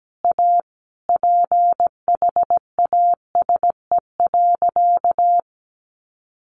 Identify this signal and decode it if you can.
Morse code, heard as a keyed tone, about 17 wpm, 700 Hz. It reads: A PHASE.
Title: Sentences from Top 1200 Words 17wpm